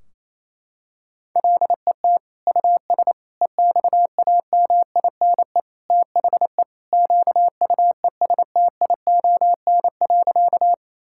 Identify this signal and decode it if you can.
Morse code, heard as a keyed tone, about 28 wpm, 700 Hz. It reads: LET US EXAMINE THE QUESTION.
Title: Sentences from Top 1600 Words 28wpm